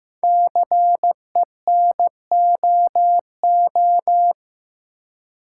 Morse code, heard as a keyed tone, about 15 wpm, 700 Hz. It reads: CENOO
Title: Code Groups - 5 Letters - 15wpm